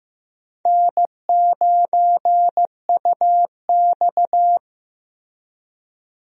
Morse code, heard as a keyed tone, about 15 wpm, 700 Hz. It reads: N9UX
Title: USA Call Signs - 4 Characters - Repeated 3x - 15wpm